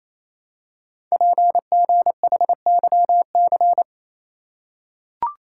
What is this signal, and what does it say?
Morse code, heard as a keyed tone, about 28 wpm, 700 Hz. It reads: PGHYC
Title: Code Groups - 5 Letters - 28wpm